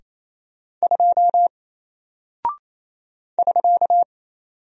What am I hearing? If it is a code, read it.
Morse code, heard as a keyed tone, about 28 wpm, 700 Hz. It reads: 2 <SK>
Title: Single Character (Letter:Number:Punctuation:Prosign) Alt 28wpm